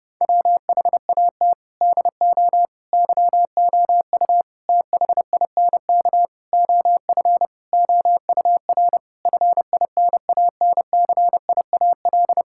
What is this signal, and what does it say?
Morse code, heard as a keyed tone, about 30 wpm, 700 Hz. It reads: WHAT DO YOU THINK OF OUR FINANCIAL
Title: Sentences from Top 1000 Words - Review Part 5 - 30wpm